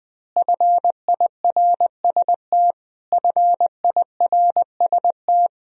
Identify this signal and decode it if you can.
Morse code, heard as a keyed tone, about 20 wpm, 700 Hz. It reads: FIRST FIRST
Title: Top 100 Words Repeated 20wpm